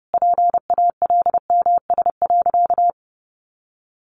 Morse code, heard as a keyed tone, about 30 wpm, 700 Hz. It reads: PALMS.
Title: A Sound Of Thunder 30wpm